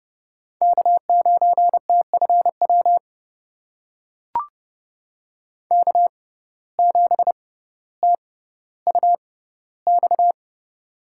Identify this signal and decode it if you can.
Morse code, heard as a keyed tone, about 30 wpm, spaced as for 15 wpm, 700 Hz. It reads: K9TFW K7TUX
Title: USA Call Signs - 5 Characters - 6x Character Spacing - 30wpm